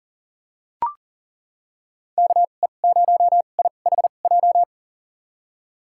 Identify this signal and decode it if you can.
Morse code, heard as a keyed tone, about 40 wpm, spaced as for 29 wpm, 700 Hz. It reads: KE0IHJ
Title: USA Call Signs - 6 Characters - 2x Character Spacing - 40wpm